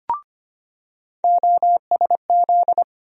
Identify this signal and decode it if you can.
Morse code, heard as a keyed tone, about 25 wpm, 700 Hz. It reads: OSZ
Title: USA Call Sign Postfixes - 3 Letters 25wpm